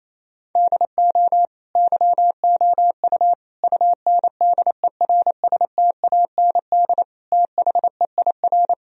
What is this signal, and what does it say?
Morse code, heard as a keyed tone, about 28 wpm, 700 Hz. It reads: DO YOU UNDERSTAND THEIR
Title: Sentences from Top 1000 Words - Review Part 3 - 28wpm